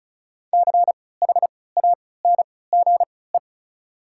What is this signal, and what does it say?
Morse code, heard as a keyed tone, about 34 wpm, spaced as for 20 wpm, 690 Hz. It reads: CHANGE
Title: 3x Character Spacing - Top 200 Words - 35wpm